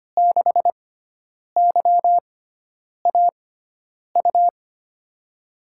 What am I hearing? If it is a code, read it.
Morse code, heard as a keyed tone, about 25 wpm, spaced as for 9 wpm, 700 Hz. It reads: 6YAU
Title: USA Call Signs - 6 Characters - 6x Character Spacing - 25wpm